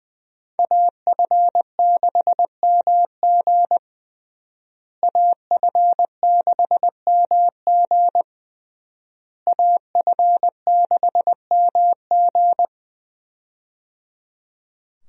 Morse code, heard as a keyed tone, about 21 wpm, 700 Hz. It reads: AF6MG AF6MG AF6MG
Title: USA Call Signs - 5 Characters - Repeated 3x - 20wpm